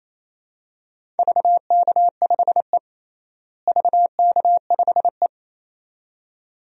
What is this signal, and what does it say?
Morse code, heard as a keyed tone, about 27 wpm, 700 Hz. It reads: VK5E VK5E
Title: DXCC-100 Call Signs - 4 Characters - Repeated 2x - 28wpm